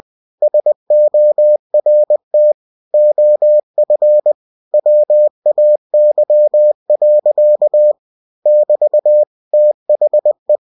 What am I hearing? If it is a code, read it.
Morse code, heard as a keyed tone, about 20 wpm, 600 Hz. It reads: SORT OF WAY. <BT> THE